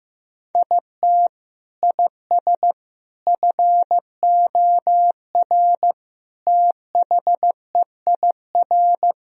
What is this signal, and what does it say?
Morse code, heard as a keyed tone, about 16 wpm, 700 Hz. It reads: IT IS FOR THEIR
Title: Sentences from Top 1000 Words - Review Part 4 - 15wpm